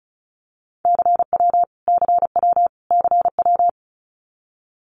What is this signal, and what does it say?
Morse code, heard as a keyed tone, about 34 wpm, 700 Hz. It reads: CW CW CW
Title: Abbreviations Repeated 3x - 35wpm